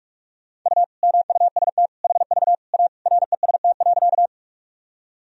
Morse code, heard as a keyed tone, about 42 wpm, 700 Hz. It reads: U MUST HV A REST.
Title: Sentences from Top 500 Words and Abbreviations 45wpm